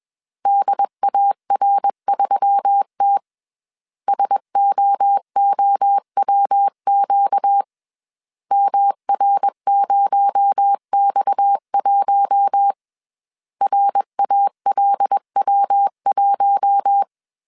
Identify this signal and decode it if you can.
Morse code, heard as a keyed tone, about 21 wpm, 800 Hz. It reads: DAR3T SOOWQ MR0X1 RALW1
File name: CW / DET / 1 6 1500C S B 20wpm 2048